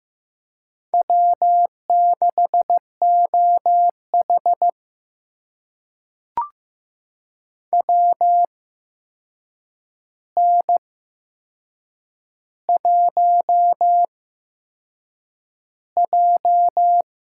Morse code, heard as a keyed tone, about 15 wpm, spaced as for 6 wpm, 700 Hz. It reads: W6OH WN1J
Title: USA Call Signs - 4 Characters - 8x Character Spacing - 15wpm